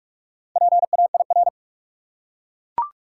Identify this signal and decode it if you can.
Morse code, heard as a keyed tone, about 41 wpm, 700 Hz. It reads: PAIR
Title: Four Letter Words - 45wpm